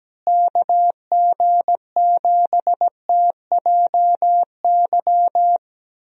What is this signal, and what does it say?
Morse code, heard as a keyed tone, about 17 wpm, 700 Hz. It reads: KG7TJY
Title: USA Call Signs - 6 Characters - Repeated 3x - 17wpm